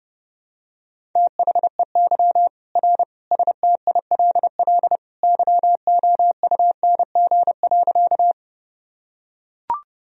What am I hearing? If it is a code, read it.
Morse code, heard as a keyed tone, about 30 wpm, 700 Hz. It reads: THEY R STILL YOUNG.
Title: Sentences from Top 400 Words and Abbreviations 30wpm